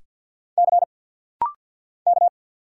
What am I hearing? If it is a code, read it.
Morse code, heard as a keyed tone, about 49 wpm, 700 Hz. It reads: C K
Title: Single Letter-Number - Rapid-Fire 50wpm